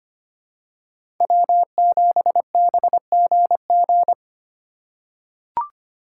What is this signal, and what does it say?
Morse code, heard as a keyed tone, about 25 wpm, 700 Hz. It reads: W7BGG
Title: USA Call Signs - 5 Characters - 3x Character Spacing - 25wpm